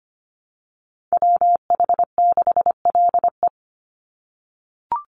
Morse code, heard as a keyed tone, about 25 wpm, 700 Hz. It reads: WH6LE
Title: NA SOTA Activator Callsigns 25wpm